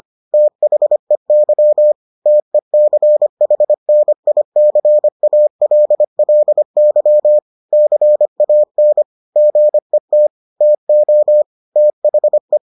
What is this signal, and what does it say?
Morse code, heard as a keyed tone, about 25 wpm, 600 Hz. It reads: THEY TECHNICALLY CAN GET TO THE